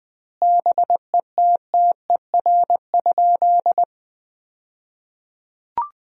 Morse code, heard as a keyed tone, about 21 wpm, 700 Hz. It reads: BETTER?
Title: 2x Word-Spacing - Sentences from Top 400 Words - 20wpm